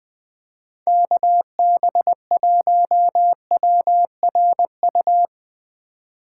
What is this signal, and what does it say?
Morse code, heard as a keyed tone, about 20 wpm, 700 Hz. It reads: KB1WRU
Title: USA Call Signs - 6 Characters - 3x Character Spacing - 20wpm